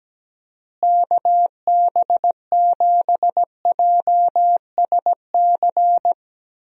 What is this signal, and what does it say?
Morse code, heard as a keyed tone, about 17 wpm, 700 Hz. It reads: KB7JSC
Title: USA Call Signs - 6 Characters - Repeated 2x - 17wpm